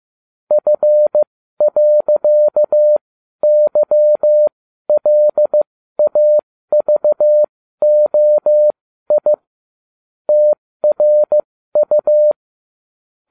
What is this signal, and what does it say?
Morse code, heard as a keyed tone, about 15 wpm, spaced as for 12 wpm, 600 Hz. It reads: F.YLAVOI TRU